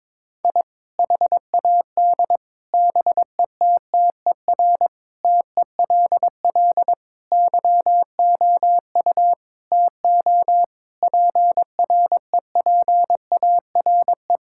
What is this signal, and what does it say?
Morse code, heard as a keyed tone, about 22 wpm, 700 Hz. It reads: I HAD BETTER TELL YOU TO PREPARE
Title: Sentences from Top 1000 Words - Review Part 6 - 22wpm